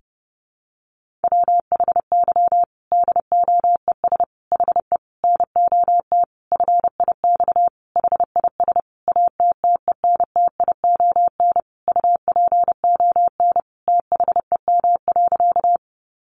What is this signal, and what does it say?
Morse code, heard as a keyed tone, about 30 wpm, 700 Hz. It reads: WHY DOES HE NOT FIX HIS ATTENTION UPON THEM.
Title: Sentences from Top 1400 Words 30wpm